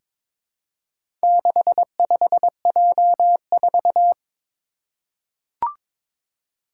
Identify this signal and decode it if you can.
Morse code, heard as a keyed tone, about 22 wpm, 700 Hz. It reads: 65J4
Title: Four Challenging Letter-Number Combinations 22wpm